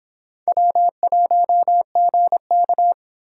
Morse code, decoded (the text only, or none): W1GK